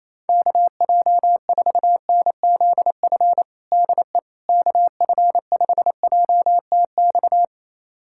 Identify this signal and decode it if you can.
Morse code, heard as a keyed tone, about 28 wpm, 700 Hz. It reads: KJ4NZF DE KF5JTX
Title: USA Callsign DE Callsign - 6 Characters - 28wpm